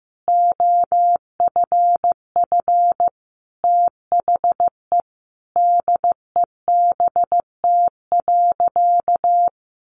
Morse code, heard as a keyed tone, about 15 wpm, 700 Hz. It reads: OFF THE DEBT.
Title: Sentences from Top 1300 Words 15wpm